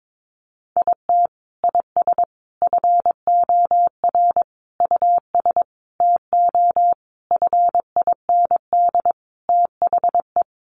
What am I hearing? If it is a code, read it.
Morse code, heard as a keyed tone, about 22 wpm, 700 Hz. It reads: IT IS FOR US TO FIND THE
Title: Sentences from Top 1300 Words 22wpm